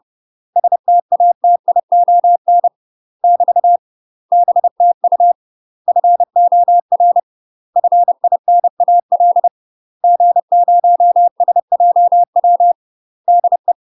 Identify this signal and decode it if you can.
Morse code, heard as a keyed tone, about 30 wpm, 700 Hz. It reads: STATION <BT> BTU FOR FINAL G0SJW DE
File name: UK MASTER QSOs TRAINING / 30WPM / UK MASTER QSOs 0022